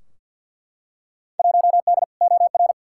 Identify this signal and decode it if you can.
Morse code, heard as a keyed tone, about 51 wpm, 700 Hz. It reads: 1D OR
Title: Field Day Exchanges - 50wpm